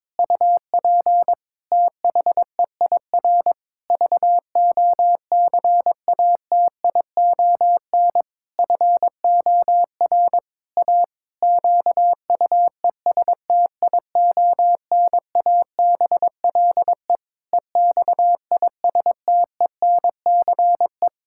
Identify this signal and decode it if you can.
Morse code, heard as a keyed tone, about 22 wpm, 700 Hz. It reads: UP THEIR VOCATION FOR A QUESTIONABLE EXISTENCE
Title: Pangram Challenge v2 - 22wpm